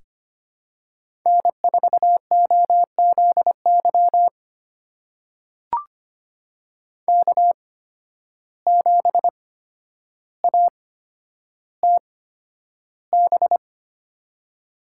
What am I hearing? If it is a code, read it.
Morse code, heard as a keyed tone, about 25 wpm, spaced as for 11 wpm, 700 Hz. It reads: N4OZY K7ATB
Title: USA Call Signs - 5 Characters - 8x Character Spacing - 25wpm